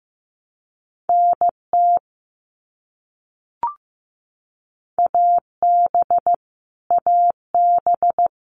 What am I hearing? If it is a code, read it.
Morse code, heard as a keyed tone, about 15 wpm, 700 Hz. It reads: NT AB AB
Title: Canadian Provinces and Territories - Repeated 3x - 15wpm